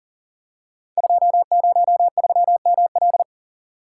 Morse code, heard as a keyed tone, about 39 wpm, 690 Hz. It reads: 203ML
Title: Code Groups - 5 Letter:Numbers - 40wpm